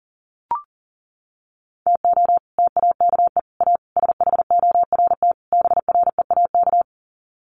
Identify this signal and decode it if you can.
Morse code, heard as a keyed tone, about 40 wpm, 700 Hz. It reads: TO TAKE A SHORT BREAK
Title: Sets of 5 Words Encore2 40wpm